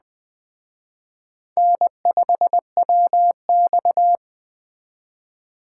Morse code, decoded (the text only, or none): N5WX